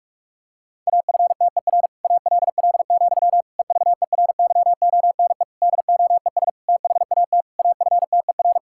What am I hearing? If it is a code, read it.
Morse code, heard as a keyed tone, about 46 wpm, 700 Hz. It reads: AFTER ALL, EVERYONE DOES THAT AFTER
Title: Sentences from Top 1000 Words - Review Part 5 - 45wpm